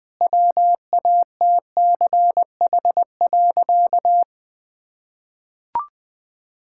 Morse code, heard as a keyed tone, about 21 wpm, 700 Hz. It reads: WATCH.
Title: Sentences from Top 1000 Words - Review Part 5 - 20wpm